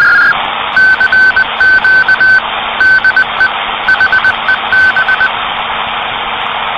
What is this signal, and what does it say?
Morse code, heard as a keyed tone, about 20 wpm, 1550 Hz, with noise. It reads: CQ DE HEB
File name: CW PACTOR CQ de HEB